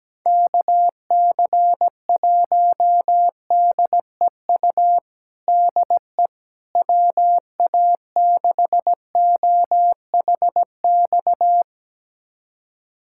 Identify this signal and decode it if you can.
Morse code, heard as a keyed tone, about 17 wpm, 700 Hz. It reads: KC1DEU DE WA6OHX
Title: USA Callsign DE Callsign - 6 Characters - 17wpm